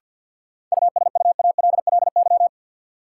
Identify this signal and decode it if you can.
Morse code, heard as a keyed tone, about 51 wpm, 700 Hz. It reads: USUALLY
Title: Sentences from Top 1400 Words - New Word Drill Alt - 50wpm